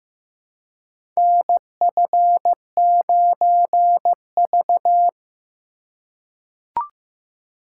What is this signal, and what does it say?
Morse code, heard as a keyed tone, about 15 wpm, 700 Hz. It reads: NF9V
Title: USA Call Signs - 4 Characters - 2x Character Spacing - 15wpm